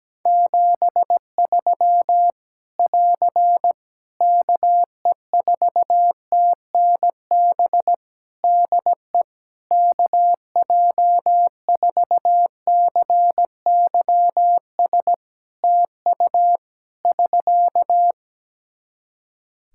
Morse code, heard as a keyed tone, about 17 wpm, 700 Hz. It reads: 73 <AR> KE4TNB DE KJ4CYS TU <SK>
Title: Practice QSO - Exchange 4 - Ending - 17wpm